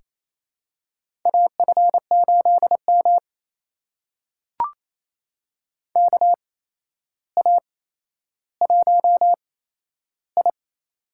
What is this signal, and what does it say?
Morse code, heard as a keyed tone, about 28 wpm, spaced as for 12 wpm, 700 Hz. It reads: AF8M KA1I
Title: USA Call Signs - 4 Characters - 8x Character Spacing - 28wpm